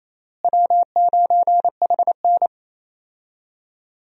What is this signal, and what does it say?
Morse code, heard as a keyed tone, about 27 wpm, 700 Hz. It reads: W9HN
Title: USA Call Signs - 4 Characters - Repeated 3x - 28wpm